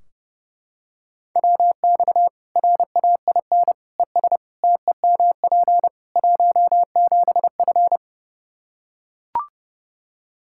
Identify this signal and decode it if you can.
Morse code, heard as a keyed tone, about 30 wpm, 700 Hz. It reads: WX RAIN ES TEMP 17F
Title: Practice QSO - Exchange 2 - Info - 30wpm